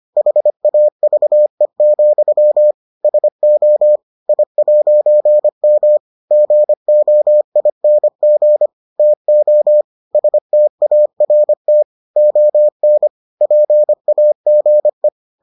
Morse code, heard as a keyed tone, about 25 wpm, 600 Hz. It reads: HAVE, SO I'M GOING TO START ON PAGE